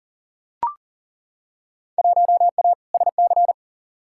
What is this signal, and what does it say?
Morse code, heard as a keyed tone, about 40 wpm, 700 Hz. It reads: 1A SC
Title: Field Day Exchanges - 40wpm